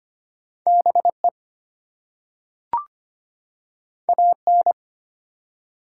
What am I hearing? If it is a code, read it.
Morse code, heard as a keyed tone, about 25 wpm, 700 Hz. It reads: BE AN
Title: Two Letter Words - 25wpm